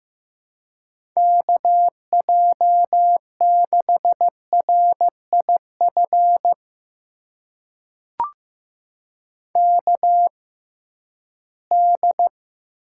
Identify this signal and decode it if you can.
Morse code, heard as a keyed tone, about 15 wpm, spaced as for 11 wpm, 700 Hz. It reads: KJ6RIF KD
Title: USA Call Signs - 6 Characters - 6x Character Spacing - 15wpm